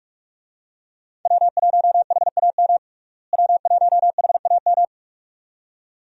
Morse code, heard as a keyed tone, about 46 wpm, 700 Hz. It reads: W1HAM W1HAM
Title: USA Call Signs - 5 Characters - Repeated 2x - 45wpm